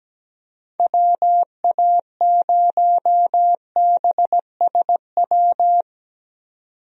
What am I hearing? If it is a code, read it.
Morse code, heard as a keyed tone, about 17 wpm, 700 Hz. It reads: WA0BSW
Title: USA Call Signs - 6 Characters - 17wpm